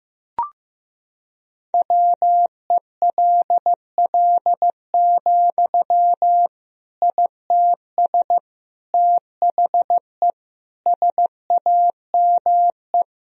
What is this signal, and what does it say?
Morse code, heard as a keyed tone, about 15 wpm, 700 Hz. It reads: WELL, ITS THE SAME
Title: Sentences from Top 1000 Words - Review Part 4 - 15wpm